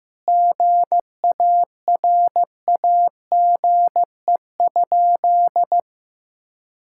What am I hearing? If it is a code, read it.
Morse code, heard as a keyed tone, about 15 wpm, 700 Hz. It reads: GARAGE?
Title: Sentences from Top 1600 Words 15wpm